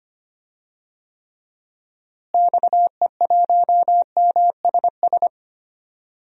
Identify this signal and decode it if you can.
Morse code, heard as a keyed tone, about 25 wpm, 700 Hz. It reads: XE1MSS
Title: DXCC-100 Call Signs - 6 Characters - Repeated 3x - 25wpm